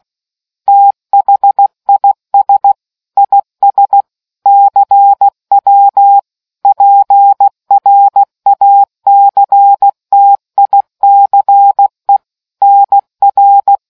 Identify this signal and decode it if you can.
Morse code, heard as a keyed tone, about 16 wpm, 800 Hz. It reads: THIS IS CW PRACTICE NR